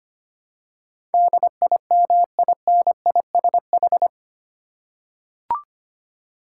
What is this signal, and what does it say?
Morse code, heard as a keyed tone, about 25 wpm, 700 Hz. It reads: DIMINISH
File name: 8 Letter Words - First 3 Letters Given - 25wpm